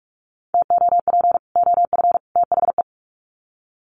Morse code, heard as a keyed tone, about 44 wpm, 700 Hz. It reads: TOP OF THE
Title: Sets of 3 Words 45wpm